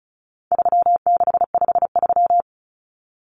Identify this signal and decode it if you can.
Morse code, heard as a keyed tone, about 34 wpm, 700 Hz. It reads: 3653
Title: Common Cut Numbers - 4-Digit 35wpm